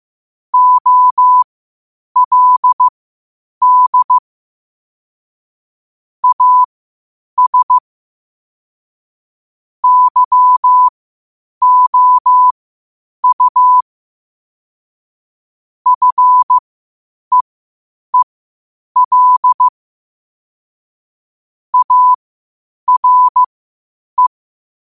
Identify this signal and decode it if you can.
Morse code, heard as a keyed tone, about 15 wpm, spaced as for 8 wpm, 1000 Hz. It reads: OLD AS YOU FEEL ARE
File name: qotdfiles / 8wpm